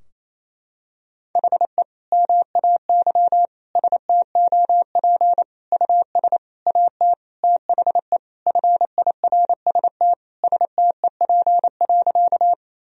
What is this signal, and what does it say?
Morse code, heard as a keyed tone, about 28 wpm, 700 Hz. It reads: HE MAY STOP US AT THE FIRST STEP.
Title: Sentences from Top 1000 Words - Review Part 2 - 28wpm